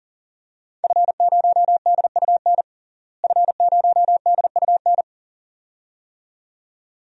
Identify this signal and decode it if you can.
Morse code, heard as a keyed tone, about 40 wpm, 700 Hz. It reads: F0DUN F0DUN
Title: DXCC-100 Call Signs - 5 Characters - Repeated 2x - 40wpm